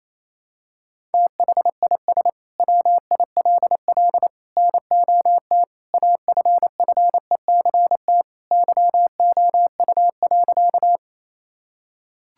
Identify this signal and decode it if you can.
Morse code, heard as a keyed tone, about 28 wpm, 700 Hz. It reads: THIS WILL NOT AFFECT YOU.
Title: Sentences from Top 1000 Words - Review Part 6 - 28wpm